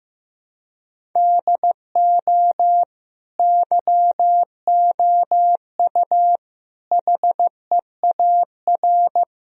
Morse code, heard as a keyed tone, about 15 wpm, 700 Hz. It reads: DO YOU HEAR
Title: Sentences from Top 1000 Words - Review Part 1 - 15wpm